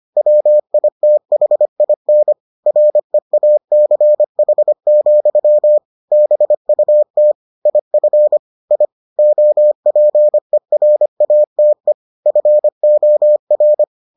Morse code, heard as a keyed tone, about 25 wpm, 600 Hz. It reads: WITHIN REACH, BUT IF I OPERATE FOR